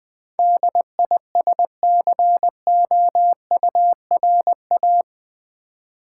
Morse code, heard as a keyed tone, about 20 wpm, 700 Hz. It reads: DISCOURA
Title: Build A Word - 4 Letters At A Time - 10-Letter Words - 20wpm